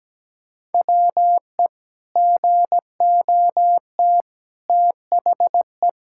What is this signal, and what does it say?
Morse code, heard as a keyed tone, about 17 wpm, 700 Hz. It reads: WE GOT THE
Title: Sentences from Top 1000 Words - Review Part 5 - 17wpm